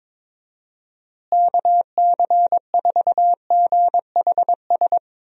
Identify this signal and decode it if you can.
Morse code, heard as a keyed tone, about 22 wpm, 700 Hz. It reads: KC4GHS
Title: USA Call Signs - 6 Characters - Repeated 2x - 22wpm